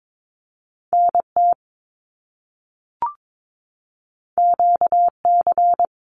Morse code, heard as a keyed tone, about 22 wpm, 700 Hz. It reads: NT QC
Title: Canadian Provinces and Territories 22wpm